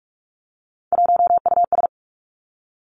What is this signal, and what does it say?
Morse code, heard as a keyed tone, about 46 wpm, 700 Hz. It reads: 1US
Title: Three Challenging Letter-Number Combinations 45wpm